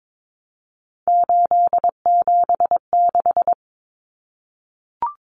Morse code, heard as a keyed tone, about 22 wpm, 700 Hz. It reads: 876
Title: Common Cut Numbers - 3-Digit 22wpm